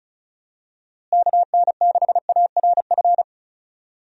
Morse code, heard as a keyed tone, about 34 wpm, 700 Hz. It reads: KN6ARF
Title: USA Call Signs - 6 Characters - Repeated 3x - 35wpm